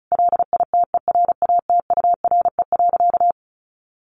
Morse code, heard as a keyed tone, about 34 wpm, 700 Hz. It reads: LITERATURE.
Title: Sentences from Top 900 Words 35wpm